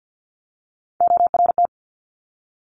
Morse code, heard as a keyed tone, about 50 wpm, 700 Hz. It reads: ORT
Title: Top Three Letter Combinations in Words 50wpm